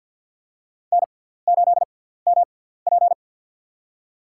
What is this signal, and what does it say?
Morse code, heard as a keyed tone, about 50 wpm, spaced as for 17 wpm, 700 Hz. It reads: N8MP
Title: USA Call Signs - 4 Characters - 6x Character Spacing - 50wpm